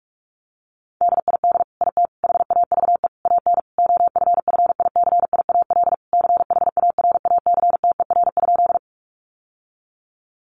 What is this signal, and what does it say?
Morse code, heard as a keyed tone, about 44 wpm, 700 Hz. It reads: DID IT HAVE AN OFFICIAL CHARACTER?
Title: Sentences from Top 1200 Words 45wpm